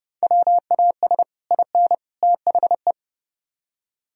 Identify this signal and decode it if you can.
Morse code, heard as a keyed tone, about 28 wpm, 700 Hz. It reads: WAS IN THE